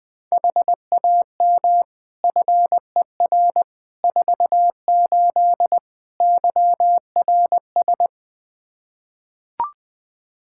Morse code, heard as a keyed tone, about 21 wpm, 700 Hz. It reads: HAM FER 48 YRS
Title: K1USN SST QSO Practice 20wpm